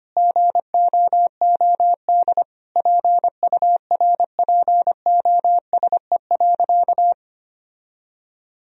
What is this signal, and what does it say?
Morse code, heard as a keyed tone, about 25 wpm, 700 Hz. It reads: GOOD PURPOSE.
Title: Sentences from Top 1000 Words - Review Part 2 - 25wpm